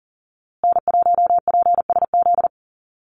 Morse code, heard as a keyed tone, about 40 wpm, 700 Hz. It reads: N1PSZ
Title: USA Call Signs Repeated 3x 40wpm